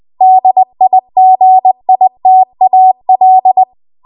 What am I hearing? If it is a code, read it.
Morse code, heard as a keyed tone, about 20 wpm, 750 Hz. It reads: DIGITAL
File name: Archive / 20 WPM / 20WPM